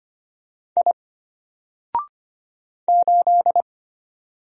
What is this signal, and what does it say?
Morse code, heard as a keyed tone, about 25 wpm, 700 Hz. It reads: I 8
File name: Single-Letter-Number-25wpm